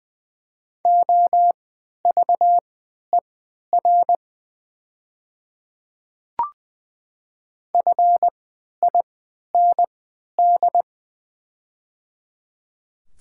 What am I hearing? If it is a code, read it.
Morse code, heard as a keyed tone, about 20 wpm, spaced as for 11 wpm, 700 Hz. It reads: OVER FIND
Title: 3x Character Spacing - Top 100 Words - 20wpm